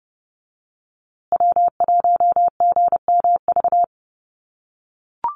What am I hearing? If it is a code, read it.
Morse code, heard as a keyed tone, about 30 wpm, 700 Hz. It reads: W1GMV
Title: USA Call Signs Repeated 3x 30wpm